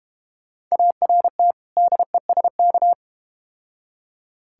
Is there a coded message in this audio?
Yes: Morse code from 0.7 to 2.9 s, heard as a keyed tone, about 32 wpm, 690 Hz. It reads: ART DESK